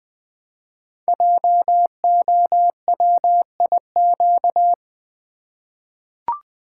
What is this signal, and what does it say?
Morse code, heard as a keyed tone, about 20 wpm, 700 Hz. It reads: JOWIQ
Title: Code Groups - 5 Letters - 20wpm